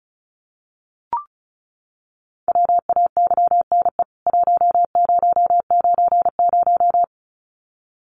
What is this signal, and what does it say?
Morse code, heard as a keyed tone, about 34 wpm, 700 Hz. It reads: WAYNE 1090
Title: CWT Exchange 35wpm